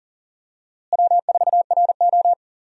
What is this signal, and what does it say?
Morse code, heard as a keyed tone, about 40 wpm, 700 Hz. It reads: W4RO
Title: USA Call Signs - 4 Characters - Repeated 2x - 40wpm